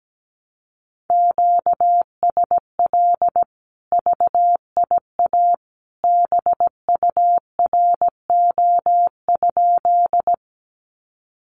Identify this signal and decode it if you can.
Morse code, heard as a keyed tone, about 17 wpm, 700 Hz. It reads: QSL VIA BURO?
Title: Common QSO Phrases 17wpm